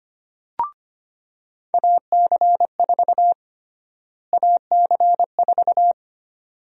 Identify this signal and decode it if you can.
Morse code, heard as a keyed tone, about 25 wpm, 700 Hz. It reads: AC4 AC4
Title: USA Call Sign Prefixes - Repeated 2x - 25wpm